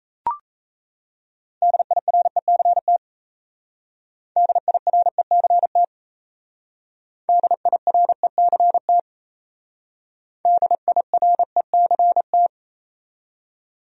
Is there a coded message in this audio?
Yes: Morse code from 1.6 to 12.5 s, heard as a keyed tone, about 35 wpm, 700 Hz. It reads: DIRECT DIRECT DIRECT DIRECT